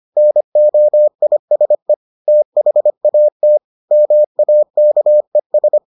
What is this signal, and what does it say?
Morse code, heard as a keyed tone, about 25 wpm, 600 Hz. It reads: NOISE THAT MAKES